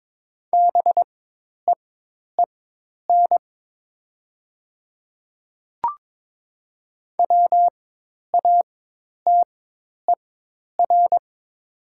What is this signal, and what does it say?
Morse code, heard as a keyed tone, about 22 wpm, spaced as for 10 wpm, 700 Hz. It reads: BEEN WATER
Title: 4x Character Spacing - Top 500 Words - 22wpm